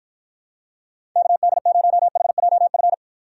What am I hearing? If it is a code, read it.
Morse code, heard as a keyed tone, about 52 wpm, 700 Hz. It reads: KD0HJF